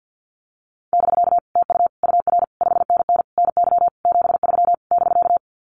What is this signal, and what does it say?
Morse code, heard as a keyed tone, about 50 wpm, 700 Hz. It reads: <BK> TU UR 5NN NY 73 <BK>
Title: Common QSO Phrases Encore 50wpm